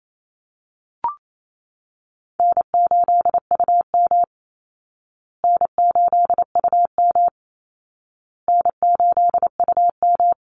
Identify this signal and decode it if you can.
Morse code, heard as a keyed tone, about 28 wpm, 700 Hz. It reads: N8UM N8UM N8UM
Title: Easy Call Signs Repeated 3x - 28wpm